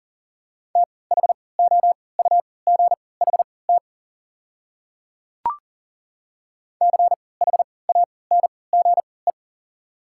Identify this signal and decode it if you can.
Morse code, heard as a keyed tone, about 40 wpm, spaced as for 23 wpm, 690 Hz. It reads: THOUGHT CHANGE
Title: 3x Character Spacing - Top 300 Words - 40wpm